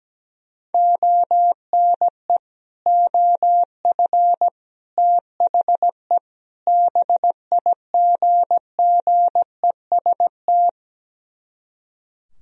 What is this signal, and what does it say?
Morse code, heard as a keyed tone, about 17 wpm, 700 Hz. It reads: ONE OF THE BIGGEST
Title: Speed-Racing - Sets of 4 Words 17wpm